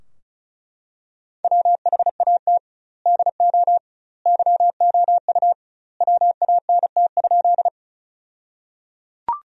Morse code, heard as a keyed tone, about 34 wpm, 700 Hz. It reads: WHAT DO YOU WANT?
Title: 2x Word-Spacing - Sentences from Top 200 Words - 35wpm